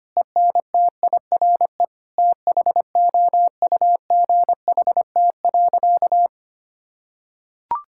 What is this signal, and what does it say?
Morse code, heard as a keyed tone, about 25 wpm, 700 Hz. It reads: ENTIRE THOUGHT.
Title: Sentences from Top 1000 Words - Review Part 5 - 25wpm